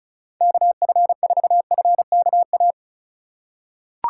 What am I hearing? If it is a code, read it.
Morse code, heard as a keyed tone, about 34 wpm, 700 Hz. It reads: KF4FKA
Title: USA Call Signs - 6 Characters - 4x Character Spacing - 35wpm